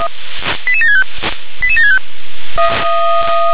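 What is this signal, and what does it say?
Five-tone selective call (ZVEI), heard as quick runs of five tones, tones of about 70 ms. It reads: 88765 88765
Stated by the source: Five-tone ZVEI call received on an SDR.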